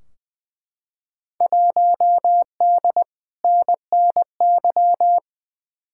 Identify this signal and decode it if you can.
Morse code, heard as a keyed tone, about 20 wpm, 700 Hz. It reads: 1D NNY
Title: Field Day Exchanges - 20wpm